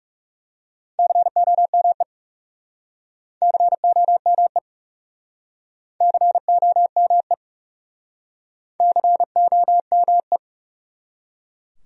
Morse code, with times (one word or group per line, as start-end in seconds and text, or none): 1.0-2.0 COME
3.4-4.6 COME
6.0-7.3 COME
8.8-10.4 COME